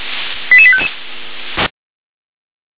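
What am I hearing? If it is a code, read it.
Five-tone selective call (ZVEI), heard as quick runs of five tones, tones of about 70 ms. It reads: 88855
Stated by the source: ZVEI five-tone selective call received by SDR.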